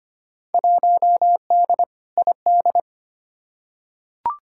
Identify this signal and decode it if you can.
Morse code, heard as a keyed tone, about 25 wpm, 700 Hz. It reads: 1D ID
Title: Field Day Exchanges - 25wpm